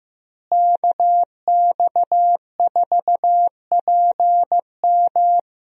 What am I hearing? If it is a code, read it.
Morse code, heard as a keyed tone, about 15 wpm, 700 Hz. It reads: KX4PM
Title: USA Call Signs - 5 Characters - Repeated 3x - 15wpm